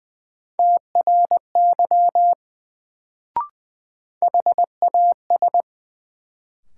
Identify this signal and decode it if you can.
Morse code, heard as a keyed tone, about 20 wpm, 700 Hz. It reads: TRY HAS
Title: Sets of 2 Words - New Word Drill - 20wpm